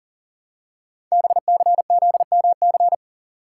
Morse code, heard as a keyed tone, about 39 wpm, 700 Hz. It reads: BCZMC
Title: Code Groups - 5 Letters - 40wpm